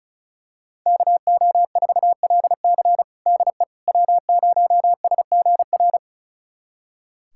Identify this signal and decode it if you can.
Morse code, heard as a keyed tone, about 36 wpm, 700 Hz. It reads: KO4LC DE W0SGR
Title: USA Callsign DE Callsign - 5 Characters - 35wpm